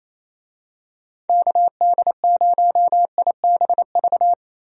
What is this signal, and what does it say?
Morse code, heard as a keyed tone, about 28 wpm, 700 Hz. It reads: KD0IBV
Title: USA Call Signs - 6 Characters - Repeated 2x - 28wpm